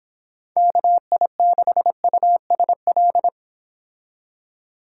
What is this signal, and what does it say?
Morse code, heard as a keyed tone, about 26 wpm, 700 Hz. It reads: KI6USL